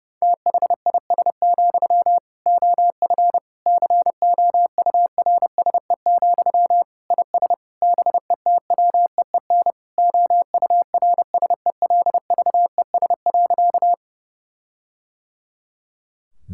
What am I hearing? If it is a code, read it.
Morse code, heard as a keyed tone, about 30 wpm, 700 Hz. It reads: THIS, OF COURSE, IS BETWEEN OURSELVES.
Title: Sentences from Top 1600 Words 30wpm